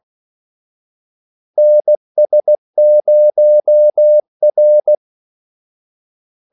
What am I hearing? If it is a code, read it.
Morse code, heard as a keyed tone, about 16 wpm, 600 Hz. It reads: NS0R
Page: Kaldesignaler 16 wpm